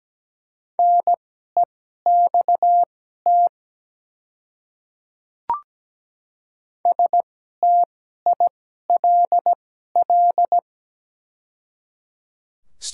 Morse code, heard as a keyed tone, about 17 wpm, spaced as for 12 wpm, 690 Hz. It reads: NEXT STILL
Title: 2x Character Spacing - Top 300 Words - 17wpm